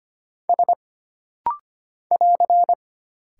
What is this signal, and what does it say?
Morse code, heard as a keyed tone, about 25 wpm, 700 Hz. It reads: S <AR>
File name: Single Character (Letter:Number:Punctuation:Prosign) - Rapid-Fire - 25wpm